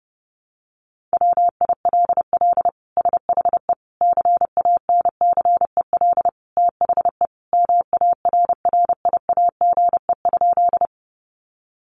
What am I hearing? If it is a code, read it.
Morse code, heard as a keyed tone, about 30 wpm, 700 Hz. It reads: WILL SHE CANCEL THE MARRIAGE?
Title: Sentences from Top 1200 Words 30wpm